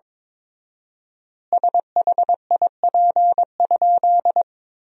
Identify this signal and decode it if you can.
Morse code, heard as a keyed tone, about 22 wpm, 700 Hz. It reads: SHIP?
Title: 4x Word-Spacing - Sentences from Top 400 Words - 22wpm